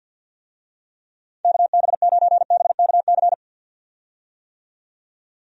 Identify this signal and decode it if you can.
Morse code, heard as a keyed tone, about 51 wpm, 700 Hz. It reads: KB9BKC
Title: USA Call Signs - 6 Characters - 50wpm